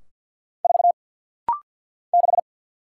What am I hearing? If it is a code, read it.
Morse code, heard as a keyed tone, about 50 wpm, 700 Hz. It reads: 4 6
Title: Single Letter-Number Alt - Rapid-Fire 50wpm